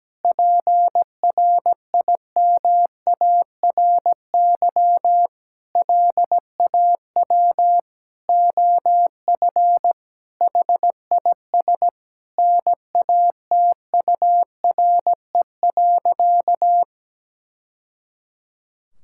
Morse code, heard as a keyed tone, about 17 wpm, 700 Hz. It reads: PRIMARY LAW OF HIS NATURE.
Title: Sentences from Top 1000 Words - Review Part 5 - 17wpm